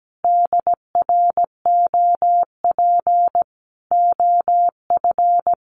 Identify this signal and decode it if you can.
Morse code, heard as a keyed tone, about 17 wpm, 700 Hz. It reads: DROP OF
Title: Sentences from Top 1500 Words 17wpm